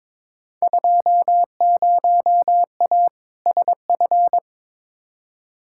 Morse code, heard as a keyed tone, about 22 wpm, 700 Hz. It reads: 20A SF
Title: Field Day Exchanges Encore - 22wpm